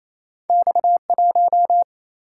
Morse code, heard as a keyed tone, about 28 wpm, 700 Hz. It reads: X 1
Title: Single-Letter Single-Number 28wpm